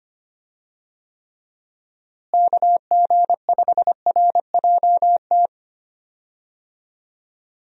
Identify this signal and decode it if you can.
Morse code, heard as a keyed tone, about 25 wpm, 700 Hz. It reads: KG5RJT
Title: USA Call Signs - 6 Characters - Repeated 2x - 25wpm